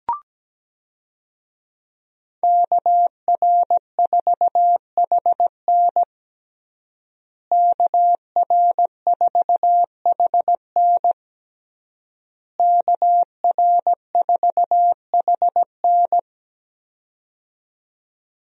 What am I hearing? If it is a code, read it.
Morse code, heard as a keyed tone, about 17 wpm, 700 Hz. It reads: KR4HN KR4HN KR4HN
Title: USA Call Signs - 5 Characters - Repeated 3x - 17wpm